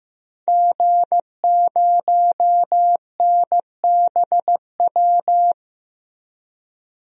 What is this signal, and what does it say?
Morse code, heard as a keyed tone, about 15 wpm, 700 Hz. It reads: G0NBW
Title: DXCC-100 Call Signs - 5 Characters - Repeated 3x - 15wpm